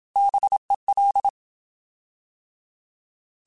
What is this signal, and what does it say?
Morse code, heard as a keyed tone, about 27 wpm, 800 Hz. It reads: BEL
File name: Alphabetical / Alphabetical mp3s / Bs @25